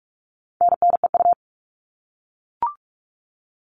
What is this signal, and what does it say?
Morse code, heard as a keyed tone, about 44 wpm, 700 Hz. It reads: NNEU
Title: All Cut Numbers - 4-Digit 45wpm